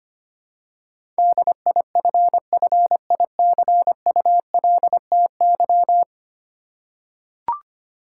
Morse code, heard as a keyed tone, about 25 wpm, 700 Hz. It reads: DIFFICULTY
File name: practice / 25wpm / Ten Letter Words - 25wpm